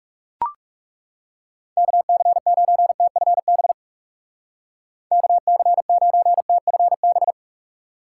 Morse code, heard as a keyed tone, about 37 wpm, 700 Hz. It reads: KC9TFB KC9TFB
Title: Speed Racing - USA Call Signs 30wpm